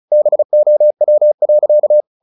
Morse code, heard as a keyed tone, about 35 wpm, 600 Hz. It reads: BOW.
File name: the old man and the sea / the old man and the sea 1 35wpm